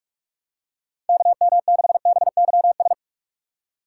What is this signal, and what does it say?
Morse code, heard as a keyed tone, about 46 wpm, 700 Hz. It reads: KM6BYS
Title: USA Call Signs - 6 Characters - 4x Character Spacing - 45wpm